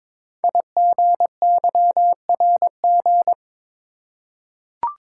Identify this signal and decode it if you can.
Morse code, heard as a keyed tone, about 21 wpm, 700 Hz. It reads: IGYRG
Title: Code Groups - 5 Letters - 22wpm